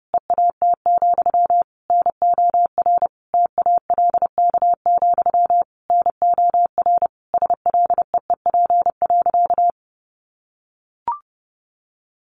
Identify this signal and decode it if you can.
Morse code, heard as a keyed tone, about 30 wpm, 700 Hz. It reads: EAT, NOR TALK, NOR SLEEP.
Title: Sentences from Top 1500 Words 30wpm